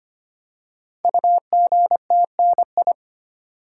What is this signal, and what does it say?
Morse code, heard as a keyed tone, about 25 wpm, 700 Hz. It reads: UGTNI
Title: Code Groups - 5 Letters - 25wpm